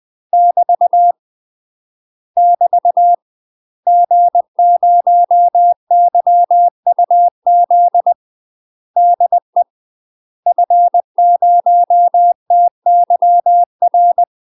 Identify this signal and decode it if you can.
Morse code, heard as a keyed tone, about 20 wpm, 700 Hz. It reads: <BT> <BT> G0YUZ DE F0TYR
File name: UK MASTER QSOs TRAINING / 20WPM / UK MASTER QSOs 0043